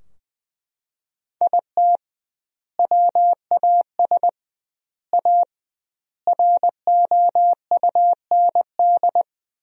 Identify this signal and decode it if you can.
Morse code, heard as a keyed tone, about 20 wpm, 700 Hz. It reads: IT WAS A ROUND
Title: 2x Word-Spacing - Sentences from Top 400 Words - 20wpm